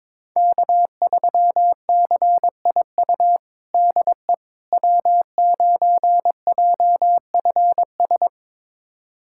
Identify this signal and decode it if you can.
Morse code, heard as a keyed tone, about 22 wpm, 700 Hz. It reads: K3CIU DE W9JFS
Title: USA Callsign DE Callsign - 5 Characters - 22wpm